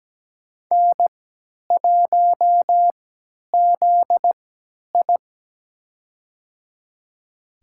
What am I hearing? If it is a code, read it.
Morse code, heard as a keyed tone, about 17 wpm, spaced as for 10 wpm, 700 Hz. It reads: N1ZI
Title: USA Call Signs - 4 Characters - 3x Character Spacing - 17wpm